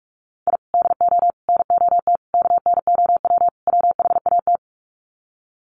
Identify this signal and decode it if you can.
Morse code, heard as a keyed tone, about 44 wpm, 700 Hz. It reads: I DO NOT KNOW WHAT
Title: Sets of 5 Words 45wpm